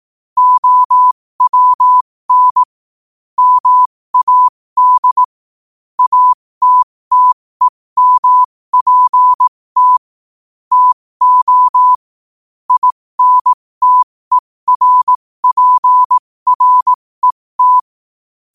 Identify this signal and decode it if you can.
Morse code, heard as a keyed tone, about 18 wpm, spaced as for 15 wpm, 1000 Hz. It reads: OWN MAD ATTEMPT TO INTERPRET
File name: qotdfiles / 15wpm